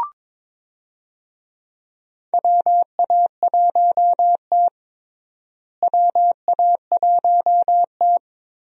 Morse code, heard as a keyed tone, about 22 wpm, 700 Hz. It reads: WA1T WA1T
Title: USA Call Signs - 4 Characters - Repeated 3x - 22wpm